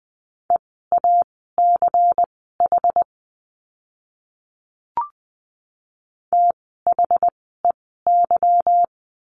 Morse code, heard as a keyed tone, about 20 wpm, spaced as for 14 wpm, 700 Hz. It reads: EACH THEY
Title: 192 Lesson 31f - Letter v - Words 20wpm